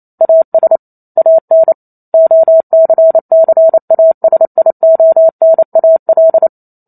Morse code, heard as a keyed tone, about 28 wpm, 650 Hz. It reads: AS AN OCCASIONAL
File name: Chapter 2 - 28 WPM